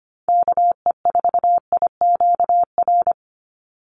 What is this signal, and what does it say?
Morse code, heard as a keyed tone, about 25 wpm, 700 Hz. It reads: KE4IQR
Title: USA Call Signs 25wpm